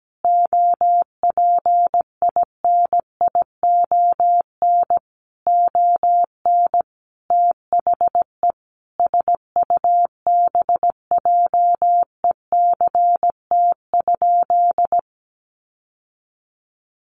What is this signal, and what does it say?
Morse code, heard as a keyed tone, about 17 wpm, 700 Hz. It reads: OPINION ON THE SUBJECT?
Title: Sentences from Top 1200 Words 17wpm